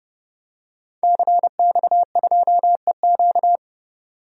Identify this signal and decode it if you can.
Morse code, heard as a keyed tone, about 31 wpm, 700 Hz. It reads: CX2EQ
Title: DXCC-100 Call Signs - 5 Characters - 30wpm